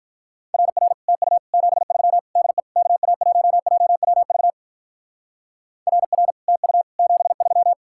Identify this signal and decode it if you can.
Morse code, heard as a keyed tone, about 44 wpm, 700 Hz. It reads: RR TU 73 DE KA1JWV RR TU 73